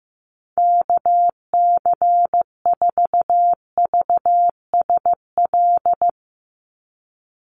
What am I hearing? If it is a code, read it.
Morse code, heard as a keyed tone, about 15 wpm, 700 Hz. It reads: KC4VSL
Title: USA Call Signs 15wpm